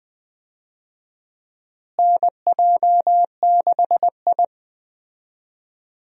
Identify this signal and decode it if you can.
Morse code, heard as a keyed tone, about 20 wpm, 700 Hz. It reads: NJ6I
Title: USA Call Signs - 4 Characters - Repeated 2x - 20wpm